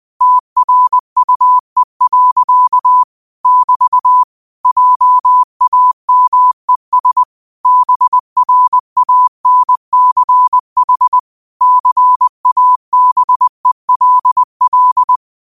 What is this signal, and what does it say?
Morse code, heard as a keyed tone, about 20 wpm, 1000 Hz. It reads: TRUE. <BT> JAMES BRANCH CABELL
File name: qotdfiles / 20wpm